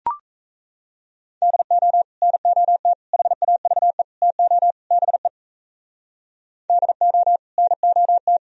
Morse code, heard as a keyed tone, about 36 wpm, 700 Hz. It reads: DO NOT HAVE TO BE DO NOT